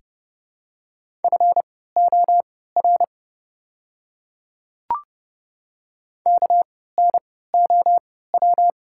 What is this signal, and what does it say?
Morse code, heard as a keyed tone, about 30 wpm, spaced as for 17 wpm, 700 Hz. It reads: FOR KNOW
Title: 3x Character Spacing - Top 100 Words - 30wpm